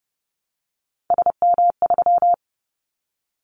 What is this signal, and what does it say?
Morse code, heard as a keyed tone, about 30 wpm, 700 Hz. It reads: SM3
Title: Callsign Prefixes for Top 25 DXCC - Repeated 3x - 30wpm